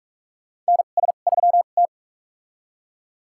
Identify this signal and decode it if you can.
Morse code, heard as a keyed tone, about 45 wpm, spaced as for 32 wpm, 700 Hz. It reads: NS3T
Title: USA Call Signs - 4 Characters - 2x Character Spacing - 45wpm